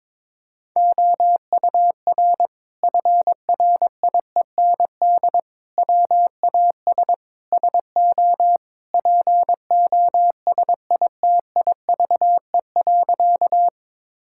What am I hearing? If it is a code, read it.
Morse code, heard as a keyed tone, about 22 wpm, 700 Hz. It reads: OUR FRIEND WAS SO POSITIVE.
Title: Sentences from Top 1000 Words - Review Part 5 - 22wpm